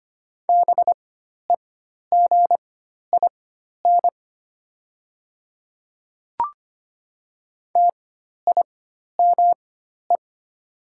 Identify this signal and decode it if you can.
Morse code, heard as a keyed tone, about 25 wpm, spaced as for 12 wpm, 700 Hz. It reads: BEGIN TIME
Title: 4x Character Spacing - Top 500 Words - 25wpm